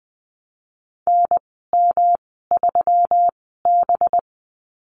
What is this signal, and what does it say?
Morse code, heard as a keyed tone, about 20 wpm, spaced as for 15 wpm, 700 Hz. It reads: NM3B
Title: 193 Lesson 31g - Letter v - Call Signs 20wpm